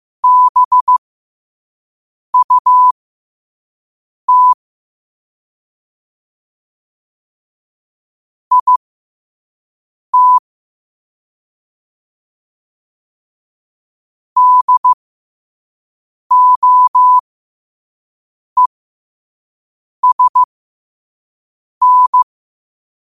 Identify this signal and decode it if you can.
Morse code, heard as a keyed tone, about 15 wpm, spaced as for 5 wpm, 1000 Hz. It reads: BUT IT DOESN
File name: qotdfiles / 5wpm